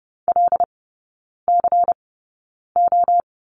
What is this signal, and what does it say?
Morse code, heard as a keyed tone, about 30 wpm, spaced as for 9 wpm, 700 Hz. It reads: LCO
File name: Lesson 16e - Letter c - Three Characters 30wpm